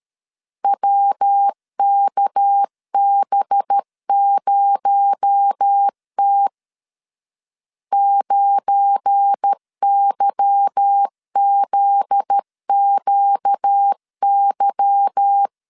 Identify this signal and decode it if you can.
Morse code, heard as a keyed tone, about 13 wpm, 800 Hz. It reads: WKB0T 9YZQY
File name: LEZIONI / 8 YJ45 / 8 Y 12wpm 2228